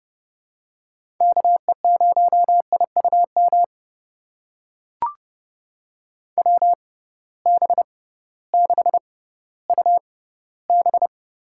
Morse code, heard as a keyed tone, about 29 wpm, spaced as for 16 wpm, 700 Hz. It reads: KE0IUM WB6UB
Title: USA Call Signs - 6 Characters - 6x Character Spacing - 30wpm